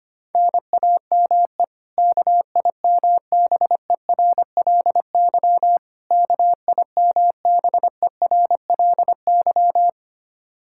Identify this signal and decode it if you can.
Morse code, heard as a keyed tone, about 25 wpm, 700 Hz. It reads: NAME KIMBERLY KIMBERLY
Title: Practice QSO - Exchange 1 - Beginning - 25wpm